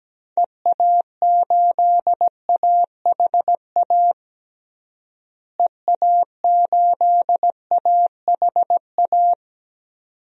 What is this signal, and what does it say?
Morse code, heard as a keyed tone, about 17 wpm, 700 Hz. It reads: EA8AHA EA8AHA
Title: DXCC-100 Call Signs - 6 Characters - Repeated 2x - 17wpm